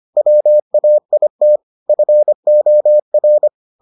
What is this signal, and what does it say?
Morse code, heard as a keyed tone, about 25 wpm, 600 Hz. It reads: WAIT FOR